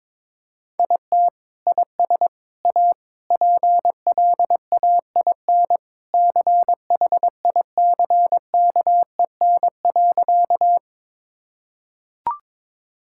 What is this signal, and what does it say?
Morse code, heard as a keyed tone, about 22 wpm, 700 Hz. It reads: IT IS A PLAIN CHICKEN.
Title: Sentences from Top 1000 Words - Review Part 5 - 22wpm